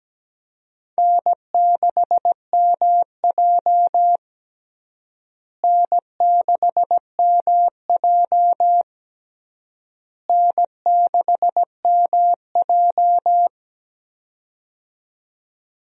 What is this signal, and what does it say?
Morse code, heard as a keyed tone, about 17 wpm, 700 Hz. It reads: N6MJ N6MJ N6MJ
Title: USA Call Signs - 4 Characters - Repeated 3x - 17wpm